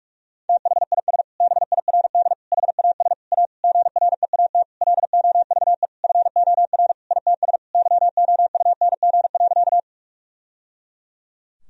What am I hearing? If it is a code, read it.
Morse code, heard as a keyed tone, about 42 wpm, 700 Hz. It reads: THIS BIRD HAS A GREAT LOVE FOR ITS YOUNG.
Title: Sentences from Top 1000 Words - Review Part 2 - 45wpm